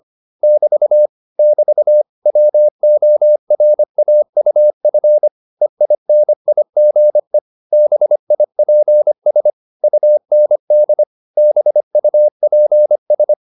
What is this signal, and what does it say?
Morse code, heard as a keyed tone, about 25 wpm, 600 Hz. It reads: <BT> <BT> WORAUF EINIGE BIPS UND BUPS